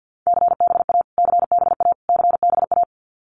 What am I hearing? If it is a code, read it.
Morse code, heard as a keyed tone, about 50 wpm, 700 Hz. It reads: CBA CBA CBA
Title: Abbreviations Repeated 3x - 50wpm